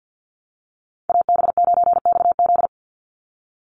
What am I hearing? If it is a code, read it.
Morse code, heard as a keyed tone, about 50 wpm, 700 Hz. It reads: AB9XZ
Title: USA Call Signs 50wpm